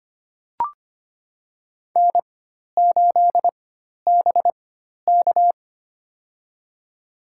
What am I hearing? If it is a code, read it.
Morse code, heard as a keyed tone, about 25 wpm, spaced as for 12 wpm, 700 Hz. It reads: N8BK
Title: USA Call Signs - 4 Characters - 4x Character Spacing - 25wpm